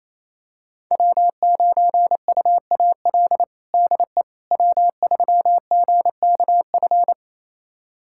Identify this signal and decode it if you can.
Morse code, heard as a keyed tone, about 28 wpm, 700 Hz. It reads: W9UAL DE W3GKF
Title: USA Callsign DE Callsign - 5 Characters - 28wpm